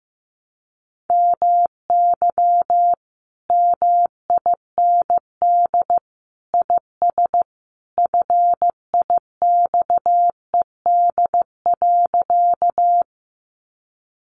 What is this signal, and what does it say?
Morse code, heard as a keyed tone, about 15 wpm, 700 Hz. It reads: MY MIND IS FIXED.
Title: Sentences from Top 1100 Words 15wpm